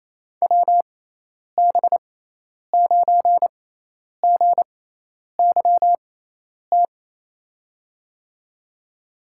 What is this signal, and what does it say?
Morse code, heard as a keyed tone, about 28 wpm, spaced as for 10 wpm, 700 Hz. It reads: WB9GYT
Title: USA Call Signs - 6 Characters - 6x Character Spacing - 28wpm